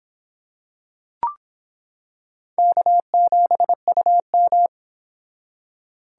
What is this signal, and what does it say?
Morse code, heard as a keyed tone, about 26 wpm, 700 Hz. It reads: K7UM